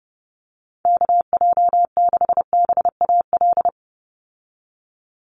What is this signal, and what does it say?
Morse code, heard as a keyed tone, about 30 wpm, 700 Hz. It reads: KJ6BAL
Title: USA Call Signs 30wpm